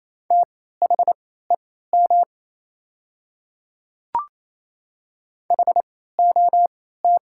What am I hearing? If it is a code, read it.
Morse code, heard as a keyed tone, about 28 wpm, spaced as for 16 wpm, 690 Hz. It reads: THEM HOT
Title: 3x Character Spacing - Top 300 Words - 28wpm